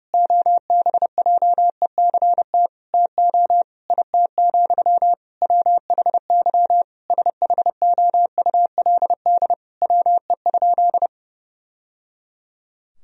Morse code, heard as a keyed tone, about 32 wpm, 700 Hz. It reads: OBJECT TO IT, WHY SHOULD WE?
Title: Sentences from Top 1000 Words - Review Part 3 - 30wpm